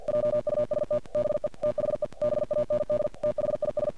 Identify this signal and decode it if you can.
Morse code, heard as a keyed tone, about 48 wpm, 590 Hz. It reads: JUST BE THE BAND THIS
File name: archived / CRSnet